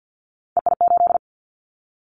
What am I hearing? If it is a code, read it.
Morse code, heard as a keyed tone, about 50 wpm, 700 Hz. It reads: EI8
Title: Callsign Prefixes for Top 50 DXCC - 50wpm